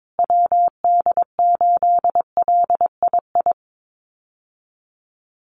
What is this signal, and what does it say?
Morse code, heard as a keyed tone, about 22 wpm, 700 Hz. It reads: WD8LII
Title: USA Call Signs Repeated 2x 22wpm